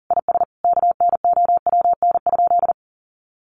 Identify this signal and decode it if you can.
Morse code, heard as a keyed tone, about 40 wpm, 700 Hz. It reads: IS KNOWN?
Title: Sentences from Top 600 Words 40wpm